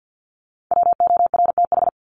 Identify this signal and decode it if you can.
Morse code, heard as a keyed tone, about 50 wpm, 700 Hz. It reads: WORTH
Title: Top 501-750 Words 50wpm